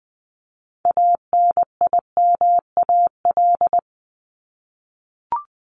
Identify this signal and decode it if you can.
Morse code, heard as a keyed tone, about 20 wpm, 700 Hz. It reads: ANIMAL
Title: Top 200 Words 20wpm